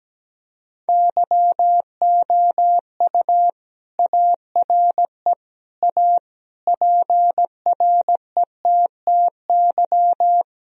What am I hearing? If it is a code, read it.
Morse code, heard as a keyed tone, about 17 wpm, 700 Hz. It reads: YOU ARE A PRETTY
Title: Sentences from Top 1000 Words - Review Part 2 - 17wpm